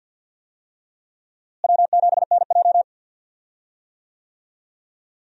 Morse code, heard as a keyed tone, about 47 wpm, 700 Hz. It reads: W7NJ
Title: USA Call Signs - 4 Characters - Repeated 2x - 50wpm